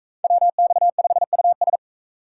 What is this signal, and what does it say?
Morse code, heard as a keyed tone, about 42 wpm, 700 Hz. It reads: WX5US